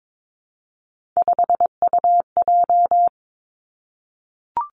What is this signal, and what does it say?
Morse code, heard as a keyed tone, about 22 wpm, 700 Hz. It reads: 5UJ
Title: Three Challenging Letter-Number Combinations 22wpm